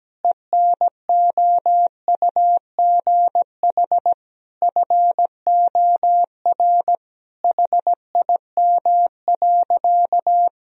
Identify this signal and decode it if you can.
Morse code, heard as a keyed tone, about 17 wpm, 700 Hz. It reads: ENOUGH FOR HIM.
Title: Sentences from Top 1000 Words - Review Part 4 - 17wpm